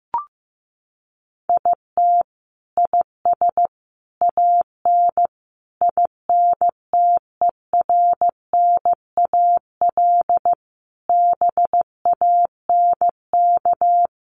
Morse code, heard as a keyed tone, about 15 wpm, 700 Hz. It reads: IT IS AN INTERNAL BANK
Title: Sentences from Top 1400 Words 15wpm